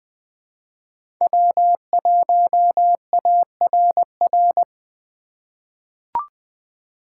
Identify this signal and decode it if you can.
Morse code, heard as a keyed tone, about 20 wpm, 700 Hz. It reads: W1ARR
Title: 2023 POTA Hunter Call Signs - 20wpm